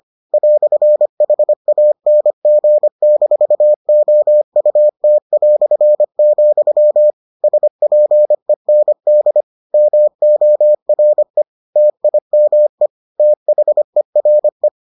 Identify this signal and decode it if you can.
Morse code, heard as a keyed tone, about 25 wpm, 600 Hz. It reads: "HANG-OUT", SPEND MORE TIME THERE